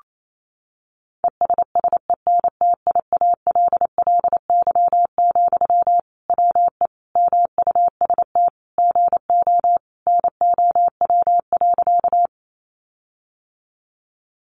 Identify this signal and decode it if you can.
Morse code, heard as a keyed tone, about 28 wpm, 700 Hz. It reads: ESSENTIALLY, WE MUST GO NOW.
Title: Sentences from Top 1300 Words 28wpm